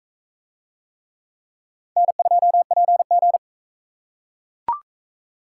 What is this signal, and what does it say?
Morse code, heard as a keyed tone, about 40 wpm, 700 Hz. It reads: N2PG